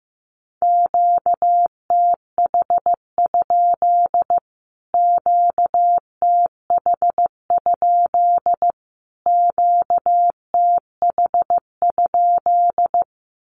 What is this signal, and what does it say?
Morse code, heard as a keyed tone, about 15 wpm, 700 Hz. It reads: QTH? QTH? QTH?
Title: Q-Codes Repeated 3x 15wpm